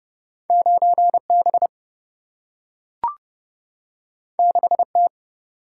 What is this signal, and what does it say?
Morse code, heard as a keyed tone, about 29 wpm, 700 Hz. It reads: 9B 6T
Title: Single-Letter Single-Number 30wpm